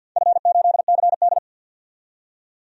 Morse code, heard as a keyed tone, about 49 wpm, 700 Hz. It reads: F8CD
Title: DXCC-100 Call Signs - 4 Characters - 50wpm